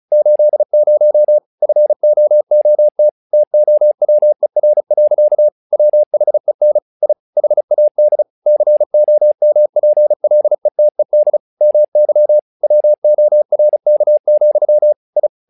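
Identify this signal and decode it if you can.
Morse code, heard as a keyed tone, about 36 wpm, 600 Hz. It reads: 80 FOOT TOWER. WHEN I HAD COMPLETED MY WORK, I
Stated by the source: Morse practice text "Tower Trouble" sent at 35 WPM, from the SKCC - Straight Key Century Club learning center audio files.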